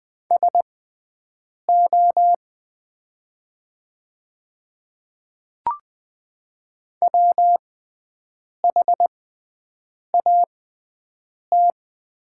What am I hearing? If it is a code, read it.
Morse code, heard as a keyed tone, about 20 wpm, spaced as for 7 wpm, 700 Hz. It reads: SO WHAT
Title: 6x Character Spacing - Top 200 Words - 20wpm